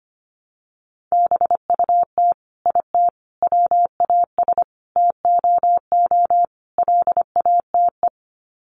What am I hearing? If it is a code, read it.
Morse code, heard as a keyed tone, about 25 wpm, 700 Hz. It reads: BUT IT WAS TOO LATE
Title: Sets of 5 Words Encore2 25wpm